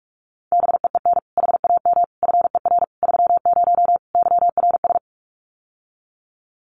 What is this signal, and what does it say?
Morse code, heard as a keyed tone, about 44 wpm, 700 Hz. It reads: BEEN HAM FER 30 YRS
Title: Common QSO Phrases Encore 45wpm